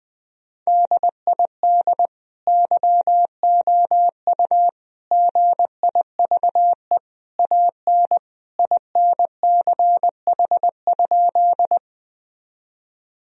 Morse code, heard as a keyed tone, about 21 wpm, 700 Hz. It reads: DID YOU GIVE AN INCH?
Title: Sentences from Top 1000 Words - Review Part 3 - 20wpm